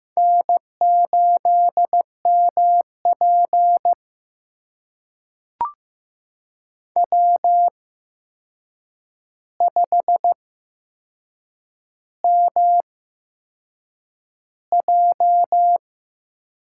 Morse code, heard as a keyed tone, about 15 wpm, spaced as for 6 wpm, 700 Hz. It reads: N8MP W5MJ
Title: USA Call Signs - 4 Characters - 8x Character Spacing - 15wpm